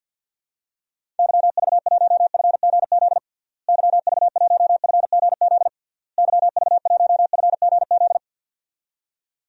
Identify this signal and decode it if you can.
Morse code, heard as a keyed tone, about 51 wpm, 700 Hz. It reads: YV1FGZ YV1FGZ YV1FGZ
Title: DXCC-100 Call Signs - 6 Characters - Repeated 3x - 50wpm